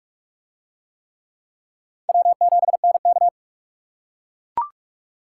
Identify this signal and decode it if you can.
Morse code, heard as a keyed tone, about 42 wpm, 700 Hz. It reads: W7NK